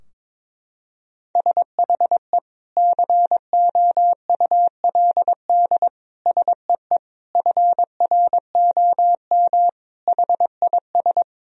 Morse code, heard as a keyed tone, about 22 wpm, 700 Hz. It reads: SHE COULD SEE FROM HIS
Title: Sentences from Top 1000 Words - Review Part 2 - 22wpm